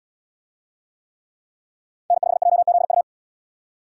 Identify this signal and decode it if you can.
Morse code, heard as a keyed tone, about 77 wpm, 700 Hz. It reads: N5JZU